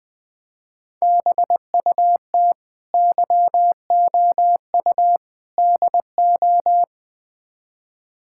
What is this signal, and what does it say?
Morse code, heard as a keyed tone, about 20 wpm, 690 Hz. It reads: BUT YOU DO
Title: Build A Phrase - 4-Word Phrases - 20wpm